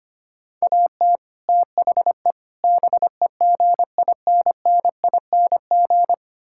Morse code, heard as a keyed tone, about 25 wpm, 700 Hz. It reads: AT THE BEGINNING
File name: practice / 25wpm / Sentences with 4-Word Phrases - 25wpm